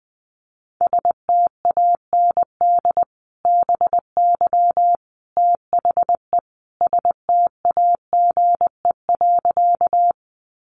Morse code, heard as a keyed tone, about 20 wpm, 700 Hz. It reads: STAND BY THE STAGE.
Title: Sentences from Top 1200 Words 20wpm